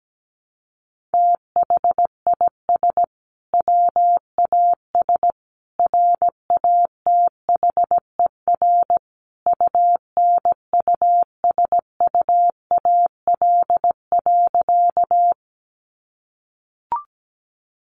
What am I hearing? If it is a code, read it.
Morse code, heard as a keyed tone, about 17 wpm, 700 Hz. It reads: THIS WAS RATHER UNUSUAL.
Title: Sentences from Top 1400 Words 17wpm